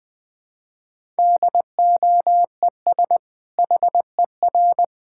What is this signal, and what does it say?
Morse code, heard as a keyed tone, about 20 wpm, 700 Hz. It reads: DOES HER
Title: Sentences from Top 1000 Words - Review Part 6 - 20wpm